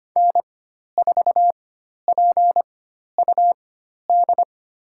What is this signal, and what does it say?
Morse code, heard as a keyed tone, about 25 wpm, spaced as for 12 wpm, 700 Hz. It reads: N4PUD
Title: USA Call Signs - 5 Characters - 4x Character Spacing - 25wpm